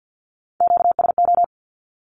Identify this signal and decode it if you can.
Morse code, heard as a keyed tone, about 50 wpm, 700 Hz. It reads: QSO
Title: Q-Codes Repeated 3x 50wpm